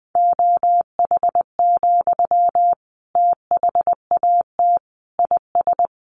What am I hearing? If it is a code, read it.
Morse code, heard as a keyed tone, about 20 wpm, 700 Hz. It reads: OH, THAT IS
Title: Sentences from Top 700 Words 20wpm